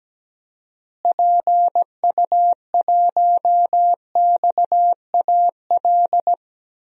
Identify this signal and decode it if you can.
Morse code, heard as a keyed tone, about 17 wpm, 700 Hz. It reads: PU1XAL
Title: DXCC-100 Call Signs - 6 Characters - Repeated 3x - 17wpm